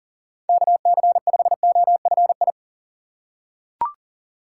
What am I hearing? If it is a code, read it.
Morse code, heard as a keyed tone, about 40 wpm, 700 Hz. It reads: KC5OFI
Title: USA Call Signs - 6 Characters - 2x Character Spacing - 40wpm